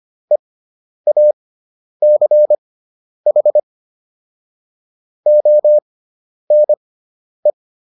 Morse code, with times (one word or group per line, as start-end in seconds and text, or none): 0.3-3.6 EACH
5.3-7.5 ONE